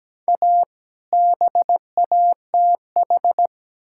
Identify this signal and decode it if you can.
Morse code, heard as a keyed tone, about 17 wpm, 700 Hz. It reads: A BATH
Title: Sentences from Top 1600 Words 17wpm